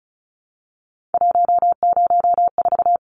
Morse code, heard as a keyed tone, about 34 wpm, 700 Hz. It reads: 104
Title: Numbers - 3-Digit 35wpm